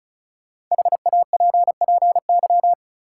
Morse code, heard as a keyed tone, about 34 wpm, 700 Hz. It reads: HAPPY
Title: Sentences from Top 1000 Words - New Word Drill - 35wpm